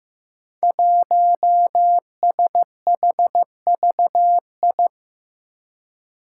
Morse code, heard as a keyed tone, about 15 wpm, 700 Hz. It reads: 1SHVI
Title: Five Challenging Letter-Number Combinations 15wpm